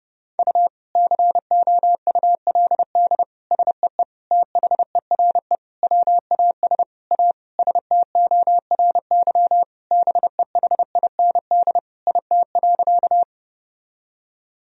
Morse code, heard as a keyed tone, about 32 wpm, 700 Hz. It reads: U COULD SEE THERE WAS A STORY BEHIND IT.
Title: Sentences from Top 500 Words and Abbreviations 30wpm